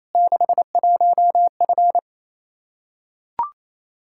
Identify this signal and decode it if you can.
Morse code, heard as a keyed tone, about 28 wpm, 700 Hz. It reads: 61F
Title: Practice QSO - Exchange 2 - Info - 28wpm